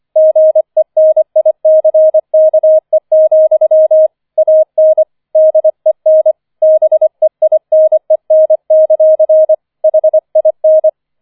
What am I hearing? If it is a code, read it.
Morse code, heard as a keyed tone, about 25 wpm, 630 Hz. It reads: GENICKE, AN DEN BEINEN; HIN
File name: 005 Fünfter Streich 25wpm